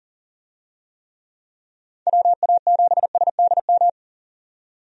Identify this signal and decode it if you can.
Morse code, heard as a keyed tone, about 37 wpm, 690 Hz. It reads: WA7SDM
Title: USA Call Signs - 6 Characters - 40wpm